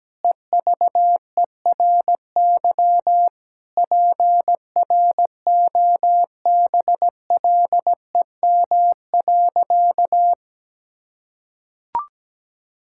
Morse code, heard as a keyed tone, about 17 wpm, 700 Hz. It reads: EVERY PROBLEM.
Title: Sentences from Top 1000 Words - Review Part 5 - 17wpm